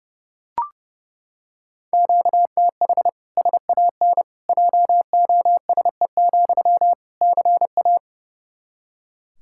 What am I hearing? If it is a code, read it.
Morse code, heard as a keyed tone, about 30 wpm, 700 Hz. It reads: QTH SAN JOSE, CA
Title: Practice QSO - Exchange 1 - Beginning - 30wpm